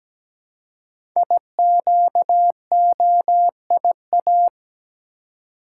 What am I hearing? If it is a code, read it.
Morse code, heard as a keyed tone, about 17 wpm, 700 Hz. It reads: IQOIA
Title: Code Groups - 5 Letters - 17wpm